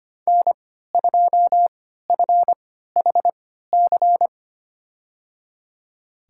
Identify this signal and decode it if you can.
Morse code, heard as a keyed tone, about 25 wpm, spaced as for 14 wpm, 700 Hz. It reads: N2FHC
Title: USA Call Signs - 5 Characters - 3x Character Spacing - 25wpm